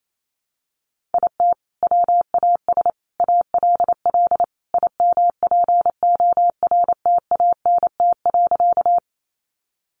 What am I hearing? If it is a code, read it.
Morse code, heard as a keyed tone, about 28 wpm, 700 Hz. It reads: IT WAS ALL IMPORTANT.
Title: Sentences from Top 600 Words 28wpm